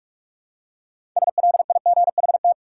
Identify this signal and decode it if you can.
Morse code, heard as a keyed tone, about 46 wpm, 700 Hz. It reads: SLIGHT
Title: Six Letter Words - 45wpm